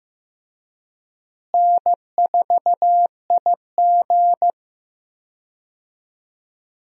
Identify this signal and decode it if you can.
Morse code, heard as a keyed tone, about 15 wpm, 700 Hz. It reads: N4IG
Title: USA Call Signs - 4 Characters - Repeated 2x - 15wpm